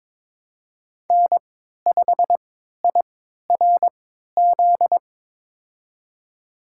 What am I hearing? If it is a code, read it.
Morse code, heard as a keyed tone, about 22 wpm, spaced as for 13 wpm, 700 Hz. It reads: N5IRZ
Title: USA Call Signs - 3x Character Spacing - 22wpm